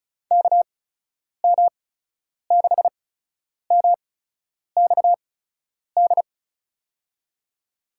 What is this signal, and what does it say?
Morse code, heard as a keyed tone, about 36 wpm, spaced as for 10 wpm, 700 Hz. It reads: KM6MXD
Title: USA Call Signs - 6 Characters - 8x Character Spacing - 35wpm